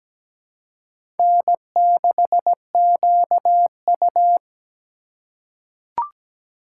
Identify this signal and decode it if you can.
Morse code, heard as a keyed tone, about 17 wpm, 700 Hz. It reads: N6QU
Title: USA Call Signs - 4 Characters - 6x Character Spacing - 17wpm